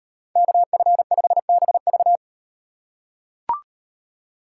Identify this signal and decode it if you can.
Morse code, heard as a keyed tone, about 38 wpm, 700 Hz. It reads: KF5BV